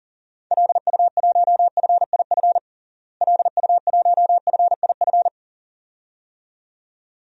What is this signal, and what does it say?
Morse code, heard as a keyed tone, about 37 wpm, 700 Hz. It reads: LU1FIF LU1FIF
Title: DXCC-100 Call Signs - 6 Characters - Repeated 2x - 40wpm